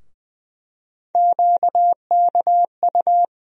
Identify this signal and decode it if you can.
Morse code, heard as a keyed tone, about 20 wpm, 700 Hz. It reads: QKU
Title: USA Call Sign Postfixes - 3 Letters 20wpm